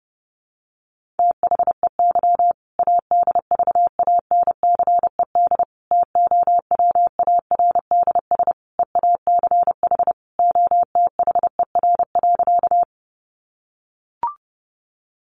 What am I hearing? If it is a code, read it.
Morse code, heard as a keyed tone, about 30 wpm, 700 Hz. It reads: THEY ADVANCED TOWARDS EACH OTHER.
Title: Sentences from Top 1200 Words 30wpm